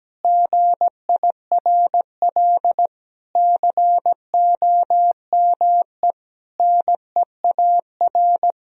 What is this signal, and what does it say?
Morse code, heard as a keyed tone, about 17 wpm, 700 Hz. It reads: GIRL COME NEAR
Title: Sentences from Top 1000 Words - Review Part 1 - 17wpm